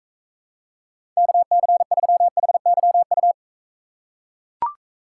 Morse code, heard as a keyed tone, about 43 wpm, 700 Hz. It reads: KC3HYU